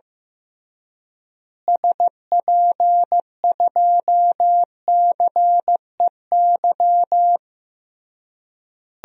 Morse code, heard as a keyed tone, about 15 wpm, 700 Hz. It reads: SP2CEY
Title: DXCC-100 Call Signs - 6 Characters - Repeated 3x - 15wpm